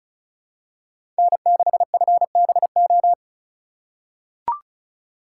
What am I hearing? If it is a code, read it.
Morse code, heard as a keyed tone, about 34 wpm, 700 Hz. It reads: N6FBO
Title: USA Call Signs - 4x Character Spacing - 35wpm